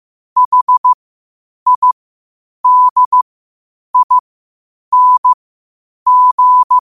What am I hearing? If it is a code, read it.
Morse code, heard as a keyed tone, about 15 wpm, spaced as for 8 wpm, 1000 Hz. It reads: HIDING